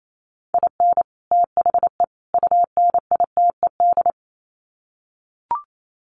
Morse code, heard as a keyed tone, about 28 wpm, 700 Hz. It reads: IN THE UNITED
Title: Sets of 3 Words 28wpm